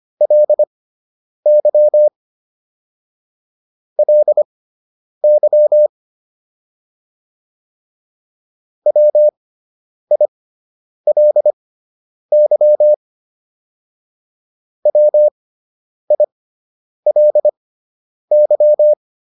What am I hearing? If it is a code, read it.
Morse code, heard as a keyed tone, about 25 wpm, spaced as for 9 wpm, 600 Hz. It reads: LY LY WILY WILY